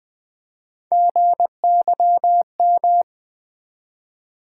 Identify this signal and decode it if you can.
Morse code, heard as a keyed tone, about 20 wpm, 700 Hz. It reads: GYM
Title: Build A Word - 1 Letter At A Time - 3-Letter Words - 20wpm